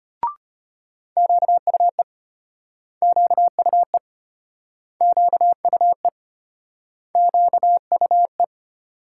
Morse code, heard as a keyed tone, about 35 wpm, 700 Hz. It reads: QUE QUE QUE QUE